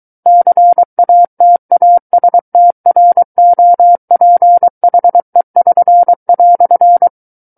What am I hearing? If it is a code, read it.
Morse code, heard as a keyed tone, about 23 wpm, 700 Hz. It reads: CATASTROPHE<SN>"
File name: Book One / 14-Chapter 14 - 23 WPM